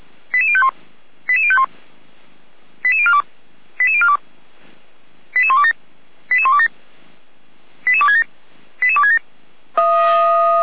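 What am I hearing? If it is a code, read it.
Five-tone selective call (ZVEI), heard as quick runs of five tones, tones of about 70 ms. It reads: 88051 88051 88042 88042 88127 88127 88267 88267
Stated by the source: ZVEI five-tone call 88235 received by SDR, Feuerwehr Orlamünde.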